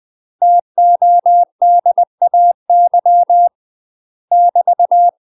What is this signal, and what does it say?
Morse code, heard as a keyed tone, about 20 wpm, 700 Hz. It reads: TODAY <BT>
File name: UK MASTER QSOs TRAINING/20WPM